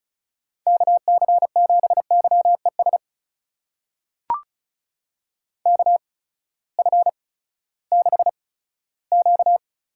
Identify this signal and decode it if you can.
Morse code, heard as a keyed tone, about 36 wpm, spaced as for 18 wpm, 700 Hz. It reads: KC7YES KF6Q
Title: USA Call Signs - 6 Characters - 8x Character Spacing - 35wpm